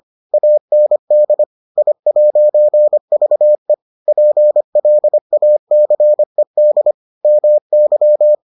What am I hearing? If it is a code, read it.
Morse code, heard as a keyed tone, about 25 wpm, 600 Hz. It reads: AND I'VE PLACED MY